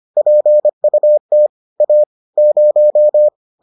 Morse code, heard as a keyed tone, about 25 wpm, 600 Hz. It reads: PUT A 0